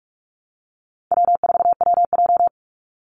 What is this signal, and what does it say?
Morse code, heard as a keyed tone, about 44 wpm, 700 Hz. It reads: W4WJ
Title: Easy Call Signs Repeated 2x - 45wpm